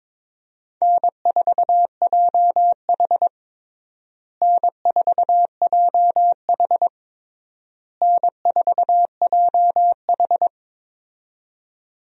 Morse code, heard as a keyed tone, about 21 wpm, 700 Hz. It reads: N4JH N4JH N4JH
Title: USA Call Signs - 4 Characters - Repeated 3x - 22wpm